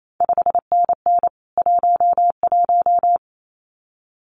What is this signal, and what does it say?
Morse code, heard as a keyed tone, about 28 wpm, 700 Hz. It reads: 5NN 11
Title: CQ WW DX Contest Exchange Encore - 28wpm